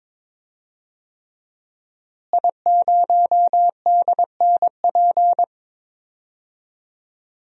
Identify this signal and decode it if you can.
Morse code, heard as a keyed tone, about 22 wpm, 700 Hz. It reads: I0DNP
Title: DXCC-100 Call Signs - 5 Characters - Repeated 3x - 22wpm